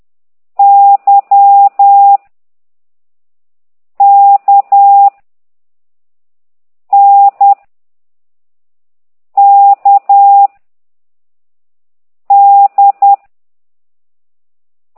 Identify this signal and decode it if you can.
Morse code, heard as a keyed tone, about 10 wpm, 800 Hz. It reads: Y K N K D